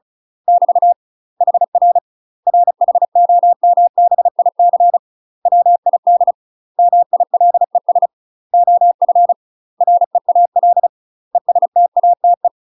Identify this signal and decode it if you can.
Morse code, heard as a keyed tone, about 34 wpm, 700 Hz. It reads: <BT> HR RHOMBIC WID MILES OF REAL ESTATE
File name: UK MASTER QSOs TRAINING / 35WPM / UK MASTER QSOs 0018